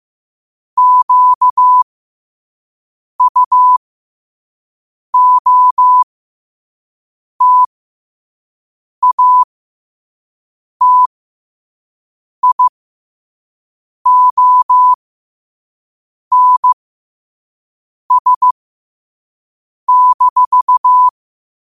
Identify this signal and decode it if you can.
Morse code, heard as a keyed tone, about 15 wpm, spaced as for 5 wpm, 1000 Hz. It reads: QUOTATIONS-
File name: qotdfiles / 5wpm